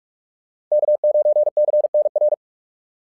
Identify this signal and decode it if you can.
Morse code, heard as a keyed tone, about 44 wpm, 690 Hz. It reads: K9CNR
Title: SOTA QSO Practice 45wpm